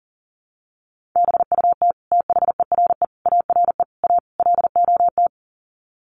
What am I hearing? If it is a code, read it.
Morse code, heard as a keyed tone, about 40 wpm, 700 Hz. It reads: BUT THERE ARE A LOT
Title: Sets of 5 Words Encore2 40wpm